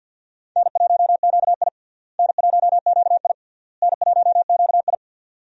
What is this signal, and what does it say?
Morse code, heard as a keyed tone, about 50 wpm, 700 Hz. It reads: N1QI N1QI N1QI
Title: USA Call Signs - 4 Characters - Repeated 3x - 50wpm